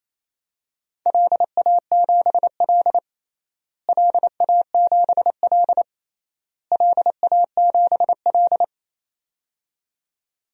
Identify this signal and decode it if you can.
Morse code, heard as a keyed tone, about 27 wpm, 700 Hz. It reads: LA7L LA7L LA7L
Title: DXCC-100 Call Signs - 4 Characters - Repeated 3x - 28wpm